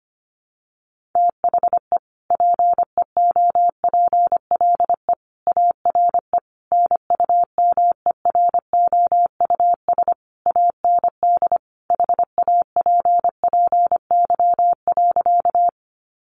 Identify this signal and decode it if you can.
Morse code, heard as a keyed tone, about 25 wpm, 700 Hz. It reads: THE PEOPLE ARE NUMEROUS AND HAPPY.
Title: Sentences from Top 1400 Words 25wpm